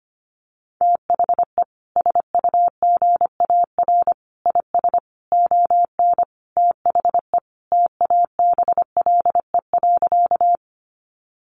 Morse code, heard as a keyed tone, about 25 wpm, 700 Hz. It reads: THE SUGAR IS ON THE TABLE.
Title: Sentences from Top 1200 Words 25wpm